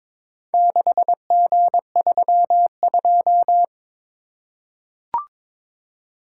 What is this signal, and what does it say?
Morse code, heard as a keyed tone, about 22 wpm, 700 Hz. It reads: 6G32
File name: practice / 22wpm / Four Letter-Number Combinations - 22wpm